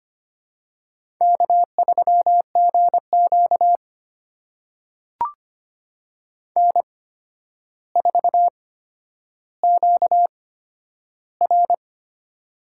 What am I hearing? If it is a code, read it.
Morse code, heard as a keyed tone, about 25 wpm, spaced as for 11 wpm, 700 Hz. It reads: K3GQ N4QR
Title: USA Call Signs - 4 Characters - 8x Character Spacing - 25wpm